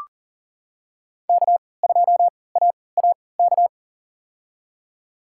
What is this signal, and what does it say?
Morse code, heard as a keyed tone, about 40 wpm, spaced as for 23 wpm, 700 Hz. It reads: K2AAK
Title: USA Call Signs - 5 Characters - 3x Character Spacing - 40wpm